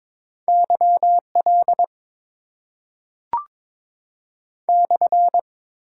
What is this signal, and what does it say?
Morse code, heard as a keyed tone, about 22 wpm, 700 Hz. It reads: YL /
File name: Top 90 QSO Elements - 6x Character Spacing - 22wpm